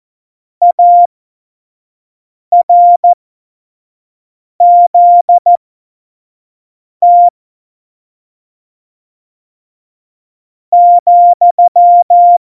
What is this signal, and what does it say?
Morse code, heard as a keyed tone, about 14 wpm, spaced as for 4 wpm, 700 Hz. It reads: ARZT,